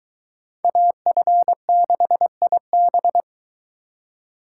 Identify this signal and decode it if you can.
Morse code, heard as a keyed tone, about 23 wpm, 700 Hz. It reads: AF6IB